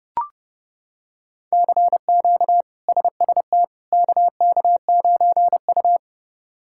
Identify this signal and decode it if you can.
Morse code, heard as a keyed tone, about 30 wpm, 700 Hz. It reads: CQ SST KK9U
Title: K1USN SST QSO Practice - 30wpm